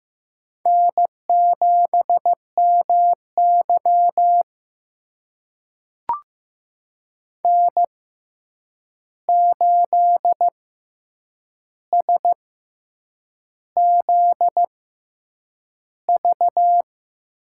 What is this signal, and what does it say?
Morse code, heard as a keyed tone, about 15 wpm, spaced as for 7 wpm, 700 Hz. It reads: N7MY N8SZV
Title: USA Call Signs - 6x Character Spacing - 15wpm